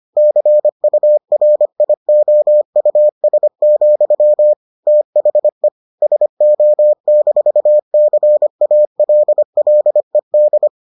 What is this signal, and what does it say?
Morse code, heard as a keyed tone, about 25 wpm, 600 Hz. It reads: CURIOUS, THE SO-CALLED